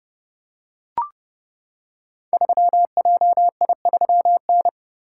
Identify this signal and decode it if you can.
Morse code, heard as a keyed tone, about 30 wpm, 700 Hz. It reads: 3JI3N
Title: Code Groups - 5 Letter:Numbers - 30wpm